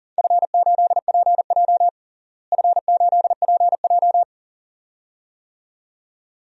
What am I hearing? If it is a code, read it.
Morse code, heard as a keyed tone, about 40 wpm, 700 Hz. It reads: F8PJ F8PJ
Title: DXCC-100 Call Signs - 4 Characters - Repeated 2x - 40wpm